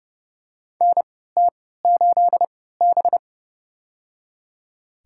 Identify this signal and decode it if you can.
Morse code, heard as a keyed tone, about 30 wpm, spaced as for 17 wpm, 700 Hz. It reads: NT8B
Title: USA Call Signs - 4 Characters - 3x Character Spacing - 30wpm